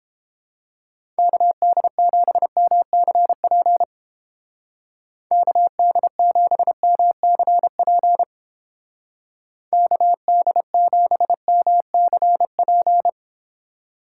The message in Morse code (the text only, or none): KD7MCP KD7MCP KD7MCP